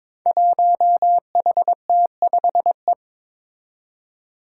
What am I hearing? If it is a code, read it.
Morse code, heard as a keyed tone, about 22 wpm, 700 Hz. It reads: 1HT5E
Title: Five Challenging Letter-Number Combinations 22wpm